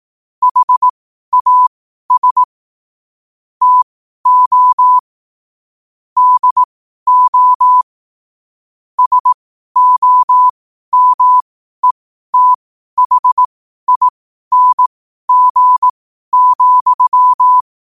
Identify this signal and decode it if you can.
Morse code, heard as a keyed tone, about 18 wpm, spaced as for 12 wpm, 1000 Hz. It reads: HAS TO DO SOMETHING,
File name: qotdfiles / 12wpm